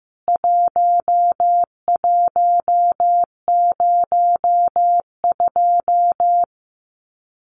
Morse code, heard as a keyed tone, about 15 wpm, 700 Hz. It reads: 1102
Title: CWT Exchange 15wpm